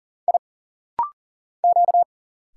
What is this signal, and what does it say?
Morse code, heard as a keyed tone, about 40 wpm, 700 Hz. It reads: I Q
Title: Single Letter-Number Alt - Rapid-Fire 40wpm